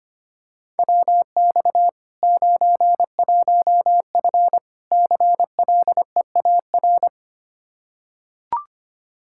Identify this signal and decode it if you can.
Morse code, heard as a keyed tone, about 25 wpm, 700 Hz. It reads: WX 91F CLEAR
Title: K1USN SST QSO Practice 25wpm